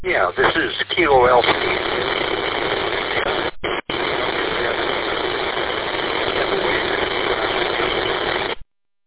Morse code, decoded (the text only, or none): none